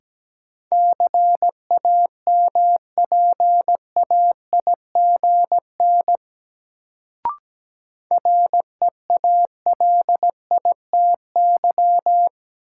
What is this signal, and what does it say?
Morse code, heard as a keyed tone, about 17 wpm, 700 Hz. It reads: CAMPAIGN REALITY
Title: Sentences from Top 1200 Words - New Word Drill Alt - 17wpm